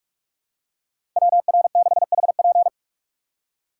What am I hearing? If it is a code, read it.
Morse code, heard as a keyed tone, about 46 wpm, 700 Hz. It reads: WR6HP
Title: USA Call Signs - 5 Characters - 3x Character Spacing - 45wpm